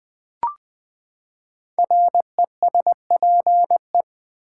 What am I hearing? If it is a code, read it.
Morse code, heard as a keyed tone, about 20 wpm, 700 Hz. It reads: RESPE
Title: Build A Word - 5 Letters At A Time - 12-Letter Words - 20wpm